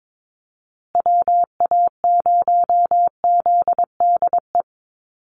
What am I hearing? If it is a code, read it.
Morse code, heard as a keyed tone, about 22 wpm, 700 Hz. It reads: WA0ZDE
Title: USA Call Signs Repeated 3x 22wpm